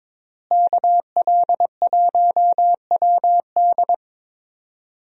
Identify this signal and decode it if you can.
Morse code, heard as a keyed tone, about 22 wpm, 700 Hz. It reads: KL1WD
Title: USA Call Signs - 5 Characters - Repeated 2x - 22wpm